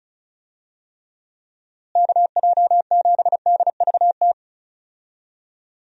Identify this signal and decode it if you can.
Morse code, heard as a keyed tone, about 34 wpm, 700 Hz. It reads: KJ7DVT
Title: USA Call Signs - 6 Characters - Repeated 3x - 35wpm